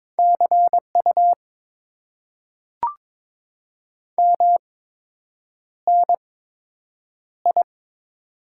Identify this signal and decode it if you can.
Morse code, heard as a keyed tone, about 22 wpm, spaced as for 8 wpm, 700 Hz. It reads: CU MNI
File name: Top 90 QSO Elements - 8x Character Spacing - 22wpm